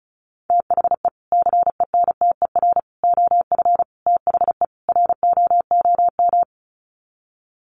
Morse code, heard as a keyed tone, about 34 wpm, 700 Hz. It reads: THE CENTER OF THE ROOM
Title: Sets of 5 Words Encore 35wpm